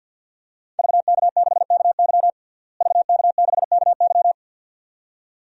Morse code, heard as a keyed tone, about 52 wpm, 700 Hz. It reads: VK6KY VK6KY
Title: DXCC-100 Call Signs - 5 Characters - Repeated 2x - 50wpm